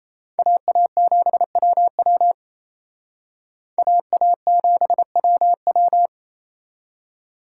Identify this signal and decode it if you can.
Morse code, heard as a keyed tone, about 35 wpm, 700 Hz. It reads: AA7WW AA7WW